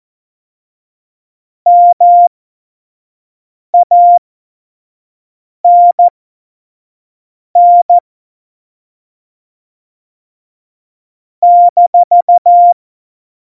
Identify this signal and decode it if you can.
Morse code, heard as a keyed tone, about 14 wpm, spaced as for 5 wpm, 700 Hz. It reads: MANN -
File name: Morsekurs / Witze / Witz 12. Dezember 2017, 25 F70